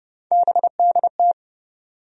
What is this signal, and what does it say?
Morse code, heard as a keyed tone, about 30 wpm, 700 Hz. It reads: BDT
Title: USA Call Sign Postfixes - 3 Letters 30wpm